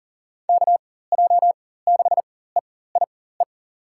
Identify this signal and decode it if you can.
Morse code, heard as a keyed tone, about 40 wpm, spaced as for 19 wpm, 700 Hz. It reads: KJ6EIE
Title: USA Call Signs - 6 Characters - 4x Character Spacing - 40wpm